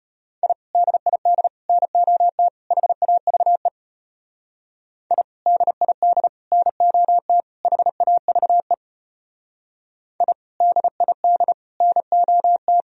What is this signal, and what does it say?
Morse code, heard as a keyed tone, about 29 wpm, 700 Hz. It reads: I DID NOT HAVE I DID NOT HAVE I DID NOT
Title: Speed-Racing - Sets of 4 Words 25wpm